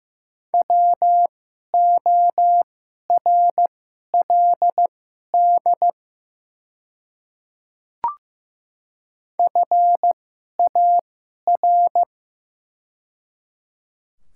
Morse code, heard as a keyed tone, about 15 wpm, spaced as for 11 wpm, 700 Hz. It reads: WORLD FAR
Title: 2x Character Spacing - Top 500 Words - 15wpm